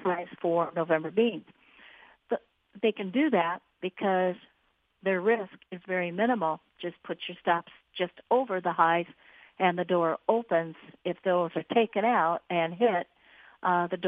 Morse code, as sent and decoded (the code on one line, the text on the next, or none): none